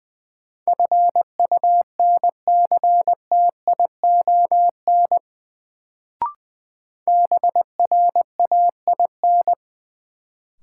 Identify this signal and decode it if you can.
Morse code, heard as a keyed tone, about 21 wpm, 700 Hz. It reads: FUNCTION BRAIN
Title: Sentences from Top 1100 Words - New Word Drill Alt - 20wpm